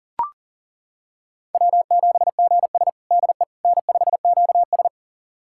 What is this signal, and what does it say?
Morse code, heard as a keyed tone, about 40 wpm, 700 Hz. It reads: W7GS DE N5QS
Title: USA Callsign DE Callsign - 4 Characters - 40wpm